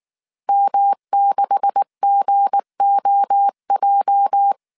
Morse code, heard as a keyed tone, about 19 wpm, 800 Hz. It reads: M6GOJ
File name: CW / DET / 1 9 1500C S A 18wpm 2459